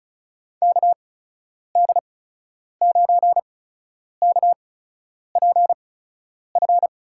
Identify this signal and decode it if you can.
Morse code, heard as a keyed tone, about 36 wpm, spaced as for 10 wpm, 700 Hz. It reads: KD9KPF
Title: USA Call Signs - 6 Characters - 8x Character Spacing - 35wpm